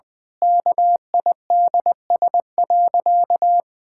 Morse code, heard as a keyed tone, about 20 wpm, 700 Hz. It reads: KIDS.
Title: Pangram Challenge v2 - 20wpm